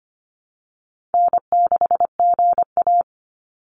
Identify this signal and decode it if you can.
Morse code, heard as a keyed tone, about 25 wpm, 700 Hz. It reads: N6GA
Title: Easy Call Signs Repeated 2x - 25wpm